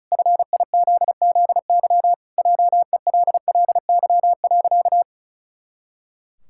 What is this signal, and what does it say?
Morse code, heard as a keyed tone, about 34 wpm, 700 Hz. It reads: FIZZY JELLY.
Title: Pangram Challenge v2 - 35wpm